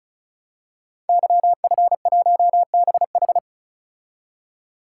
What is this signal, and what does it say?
Morse code, heard as a keyed tone, about 36 wpm, 690 Hz. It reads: YF1BH
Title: Code Groups - 5 Letter:Numbers - 35wpm